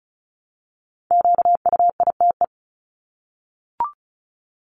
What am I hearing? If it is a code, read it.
Morse code, heard as a keyed tone, about 34 wpm, 700 Hz. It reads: QUITE
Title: Top 501-750 Words 35wpm